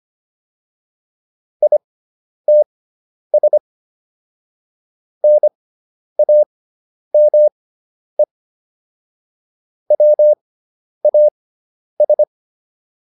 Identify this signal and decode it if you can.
Morse code, heard as a keyed tone, about 25 wpm, spaced as for 10 wpm, 600 Hz. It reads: ITS NAME WAS